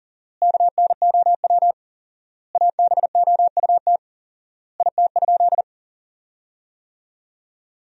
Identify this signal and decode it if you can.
Morse code, heard as a keyed tone, about 37 wpm, 700 Hz. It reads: KNOW ABOUT IT?
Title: 4x Word-Spacing - Sentences from Top 100 Words - 40wpm